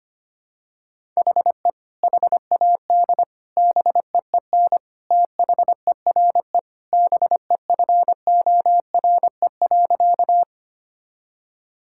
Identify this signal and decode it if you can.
Morse code, heard as a keyed tone, about 25 wpm, 700 Hz. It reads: HE HAD BEEN THERE BEFORE.
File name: practice / 25wpm / Sentences with 3-Word Phrases - 25wpm